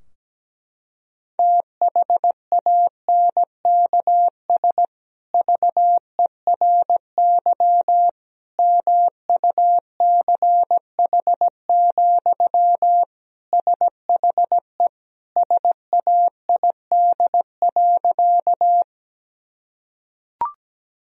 Morse code, heard as a keyed tone, about 17 wpm, 700 Hz. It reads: THANKS VERY MUCH, SHE SAID.
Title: Sentences from Top 1000 Words - Review Part 3 - 17wpm